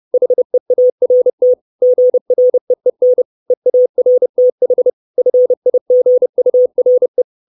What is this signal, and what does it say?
Morse code, heard as a keyed tone, about 30 wpm, 500 Hz. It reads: HEART GREEN EARTH FIGURE
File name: 632 Words 30WPM